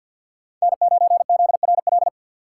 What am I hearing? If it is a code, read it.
Morse code, heard as a keyed tone, about 53 wpm, 700 Hz. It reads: N9ZRL
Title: USA Call Signs - 5 Characters - 6x Character Spacing - 50wpm